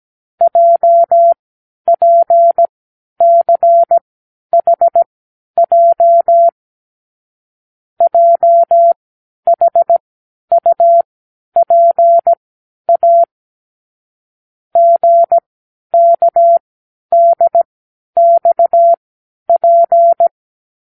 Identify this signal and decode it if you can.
Morse code, heard as a keyed tone, about 17 wpm, spaced as for 10 wpm, 680 Hz. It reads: JPCHJ JHUPA GKDXP
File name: k7qo / Lesson 36 (A-X)